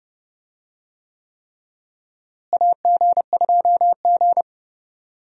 Morse code, heard as a keyed tone, about 32 wpm, 700 Hz. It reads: AG2G
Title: USA Call Signs - 4 Characters - Repeated 3x - 30wpm